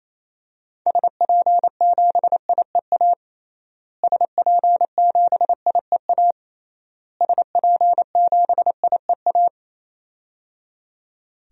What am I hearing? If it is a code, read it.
Morse code, heard as a keyed tone, about 28 wpm, 700 Hz. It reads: SP7IEA SP7IEA SP7IEA
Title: DXCC-100 Call Signs - 6 Characters - Repeated 3x - 28wpm